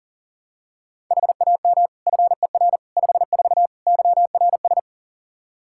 Morse code, heard as a keyed tone, about 40 wpm, 700 Hz. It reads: HAM FER 54 YRS
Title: Practice QSO - Exchange 3 - More Info - 40wpm